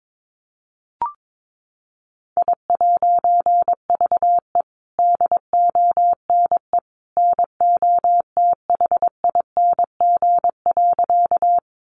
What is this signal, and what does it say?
Morse code, heard as a keyed tone, about 22 wpm, 700 Hz. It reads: I'VE DONE NOTHING.
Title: A Sound Of Thunder 22wpm